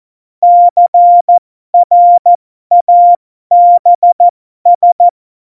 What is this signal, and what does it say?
Morse code, heard as a keyed tone, about 14 wpm, spaced as for 12 wpm, 700 Hz. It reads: CRABS
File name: Morsekurs / Witze